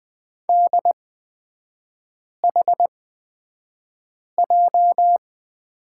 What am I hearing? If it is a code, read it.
Morse code, heard as a keyed tone, about 20 wpm, spaced as for 5 wpm, 700 Hz. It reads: DHJ
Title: 229 Lesson 37e - Letter x - Three Characters 20wpm